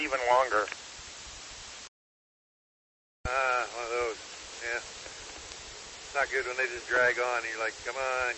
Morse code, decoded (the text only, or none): none